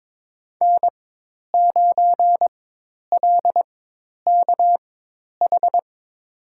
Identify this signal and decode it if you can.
Morse code, heard as a keyed tone, about 22 wpm, spaced as for 10 wpm, 700 Hz. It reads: N9LKH
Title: USA Call Signs - 5 Characters - 4x Character Spacing - 22wpm